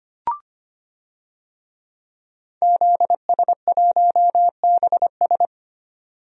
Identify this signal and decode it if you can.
Morse code, heard as a keyed tone, about 25 wpm, 700 Hz. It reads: ZS1BS
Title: DXCC-100 Call Signs - 5 Characters - 25wpm